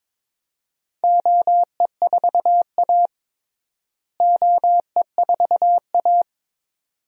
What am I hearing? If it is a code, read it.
Morse code, heard as a keyed tone, about 22 wpm, 700 Hz. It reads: OE4A OE4A
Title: DXCC-100 Call Signs - 4 Characters - Repeated 2x - 22wpm